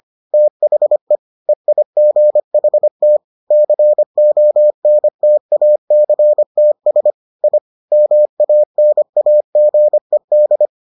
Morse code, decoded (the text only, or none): THE EIGHT CONTACTS I MANAGED